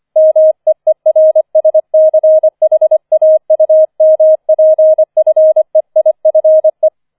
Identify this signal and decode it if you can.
Morse code, heard as a keyed tone, about 25 wpm, 630 Hz. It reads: MEERSCHAUMPFEIFE
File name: 004 Vierter Streich 25wpm